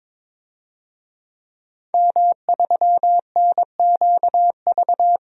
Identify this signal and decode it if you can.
Morse code, heard as a keyed tone, about 22 wpm, 700 Hz. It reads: M3NQV
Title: DXCC-100 Call Signs - 5 Characters - 22wpm